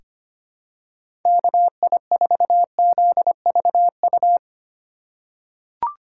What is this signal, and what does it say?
Morse code, heard as a keyed tone, about 25 wpm, 700 Hz. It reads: KI4ZVU
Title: K1USN SST QSO Practice - 25wpm